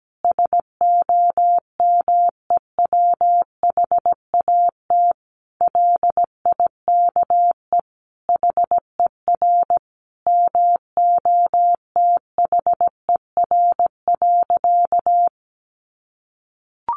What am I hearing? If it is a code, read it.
Morse code, heard as a keyed tone, about 17 wpm, 700 Hz. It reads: SOMEWHAT LIKE HER MOTHER.
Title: Sentences from Top 1300 Words 17wpm